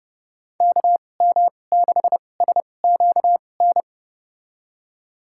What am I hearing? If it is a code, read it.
Morse code, heard as a keyed tone, about 30 wpm, spaced as for 22 wpm, 700 Hz. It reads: KM6SQN
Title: USA Call Signs - 2x Character Spacing - 30wpm